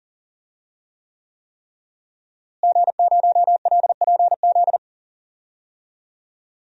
Morse code, heard as a keyed tone, about 40 wpm, 700 Hz. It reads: G0LPZ
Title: DXCC-100 Call Signs - 5 Characters - Repeated 3x - 40wpm